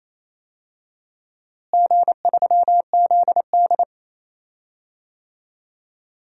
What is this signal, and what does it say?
Morse code, heard as a keyed tone, about 28 wpm, 700 Hz. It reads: G3ZD
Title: DXCC-100 Call Signs - 4 Characters - Repeated 3x - 28wpm